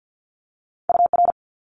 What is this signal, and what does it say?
Morse code, heard as a keyed tone, about 50 wpm, 700 Hz. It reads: UR
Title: Multi-letter Phonemes Quick Fire 50wpm